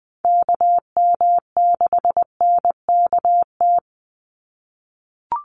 Morse code, heard as a keyed tone, about 20 wpm, 700 Hz. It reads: KM6NKT
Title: USA Call Signs 20wpm